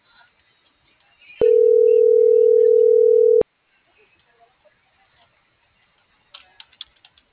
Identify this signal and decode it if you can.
Ringback tone (North America), 440 + 480 Hz.